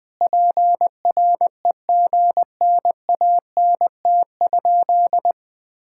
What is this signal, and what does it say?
Morse code, heard as a keyed tone, about 19 wpm, 700 Hz. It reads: PREGNANT?
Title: Sentences from Top 1000 Words - Review Part 6 - 20wpm